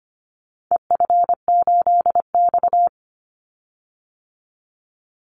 Morse code, heard as a keyed tone, about 25 wpm, 700 Hz. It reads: EF8X
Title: Call Signs - 25wpm